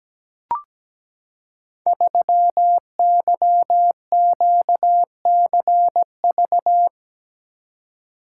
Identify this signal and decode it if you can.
Morse code, heard as a keyed tone, about 17 wpm, 700 Hz. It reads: 3YQCV
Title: Code Groups - 5 Letter:Numbers - 17wpm